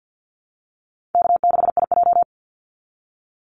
Morse code, heard as a keyed tone, about 50 wpm, 700 Hz. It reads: K6IJ
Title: Easy Call Signs - 50wpm